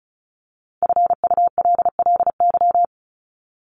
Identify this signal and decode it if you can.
Morse code, heard as a keyed tone, about 34 wpm, 700 Hz. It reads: FULLY
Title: LY Suffix - Top 100 Words with Base Words 35wpm